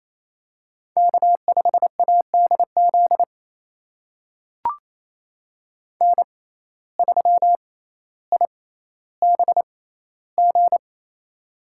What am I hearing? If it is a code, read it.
Morse code, heard as a keyed tone, about 27 wpm, spaced as for 14 wpm, 700 Hz. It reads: K5ADZ N3IBG
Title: USA Call Signs - 5 Characters - 6x Character Spacing - 28wpm